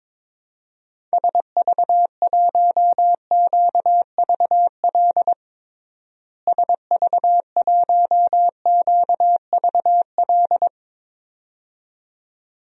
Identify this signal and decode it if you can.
Morse code, heard as a keyed tone, about 22 wpm, 700 Hz. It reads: SV1QVL SV1QVL
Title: DXCC-100 Call Signs - 6 Characters - Repeated 2x - 22wpm